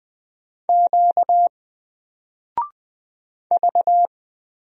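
Morse code, heard as a keyed tone, about 20 wpm, 700 Hz. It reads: Q V
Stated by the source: Single letter and number Morse drill at 20 wpm, from N4LCD.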